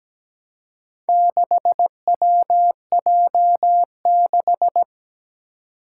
Morse code, heard as a keyed tone, about 17 wpm, 700 Hz. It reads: 6WJ6
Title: Four Challenging Letter-Number Combinations 17wpm